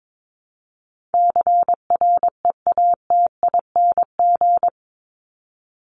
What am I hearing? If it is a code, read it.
Morse code, heard as a keyed tone, about 22 wpm, 700 Hz. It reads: CREATING
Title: ING Suffix - Top 100 Words with Base Words 22wpm